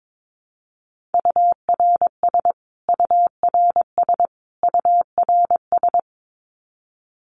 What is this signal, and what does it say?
Morse code, heard as a keyed tone, about 22 wpm, 700 Hz. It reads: URS URS URS
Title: Abbreviations Repeated 3x - 22wpm